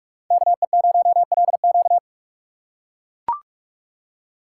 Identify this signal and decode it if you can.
Morse code, heard as a keyed tone, about 41 wpm, 690 Hz. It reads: KE0LQ
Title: USA Call Signs - 5 Characters - 6x Character Spacing - 45wpm